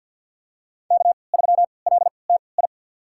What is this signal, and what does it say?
Morse code, heard as a keyed tone, about 47 wpm, spaced as for 27 wpm, 700 Hz. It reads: K3LTI
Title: USA Call Signs - 3x Character Spacing - 50wpm